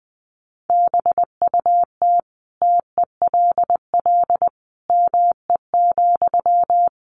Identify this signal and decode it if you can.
Morse code, heard as a keyed tone, about 19 wpm, 700 Hz. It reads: BUT TELL ME,
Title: Sentences from Top 1500 Words 20wpm